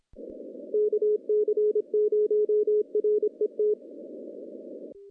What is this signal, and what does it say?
Morse code, heard as a keyed tone, about 25 wpm, 420 Hz. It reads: KC0RET
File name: contests / CW-SS-k5tr- -1400